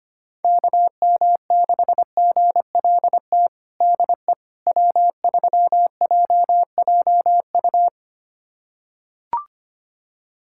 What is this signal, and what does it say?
Morse code, heard as a keyed tone, about 25 wpm, 700 Hz. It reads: KM6GLT DE W3JJU
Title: USA Callsign DE Callsign 25wpm